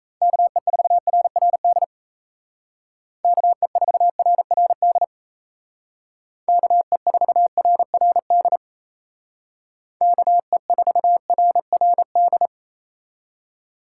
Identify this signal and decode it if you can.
Morse code, heard as a keyed tone, about 35 wpm, 700 Hz. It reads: KE4RRD KE4RRD KE4RRD KE4RRD
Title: Speed Racing - USA 2x3 Call Signs 28wpm